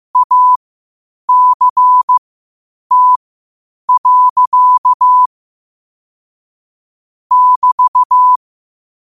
Morse code, heard as a keyed tone, about 15 wpm, spaced as for 8 wpm, 1000 Hz. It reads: ACT. <BT>
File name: qotdfiles / 8wpm